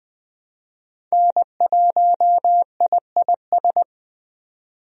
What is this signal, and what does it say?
Morse code, heard as a keyed tone, about 20 wpm, 700 Hz. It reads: N1IIS
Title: USA Call Signs - 5 Characters - Repeated 3x - 20wpm